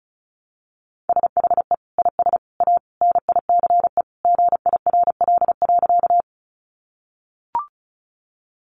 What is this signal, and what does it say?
Morse code, heard as a keyed tone, about 34 wpm, 700 Hz. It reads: SHE IS A NICE GIRL.
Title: Sentences from Top 1100 Words 35wpm